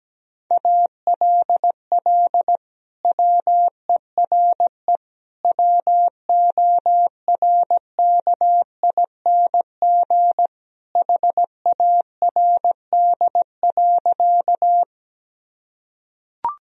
Morse code, heard as a keyed tone, about 17 wpm, 700 Hz. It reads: ALL WERE WORKING HARD.
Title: Sentences from Top 1000 Words - Review Part 4 - 17wpm